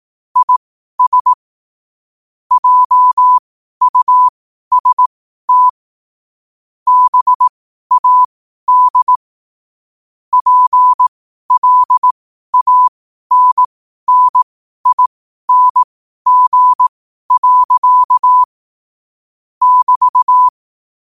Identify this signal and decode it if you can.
Morse code, heard as a keyed tone, about 18 wpm, spaced as for 12 wpm, 1000 Hz. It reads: IS JUST BAD PLANNING. <BT>
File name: qotdfiles / 12wpm